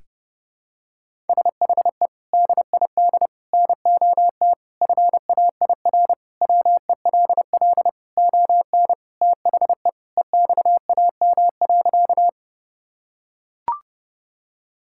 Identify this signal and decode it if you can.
Morse code, heard as a keyed tone, about 30 wpm, 700 Hz. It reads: SHE DID NOT FAIR WELL ON THE EXAM.
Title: Sentences from Top 1000 Words - Review Part 5 - 30wpm